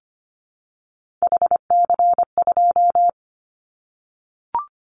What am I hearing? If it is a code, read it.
Morse code, heard as a keyed tone, about 25 wpm, 700 Hz. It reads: HC2
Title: Callsign Prefixes for Top 100 DXCC - 25wpm